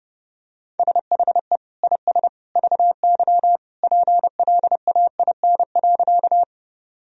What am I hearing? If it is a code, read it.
Morse code, heard as a keyed tone, about 30 wpm, 700 Hz. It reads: SHE IS VY PLAIN.
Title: Sentences from Top 400 Words and Abbreviations 30wpm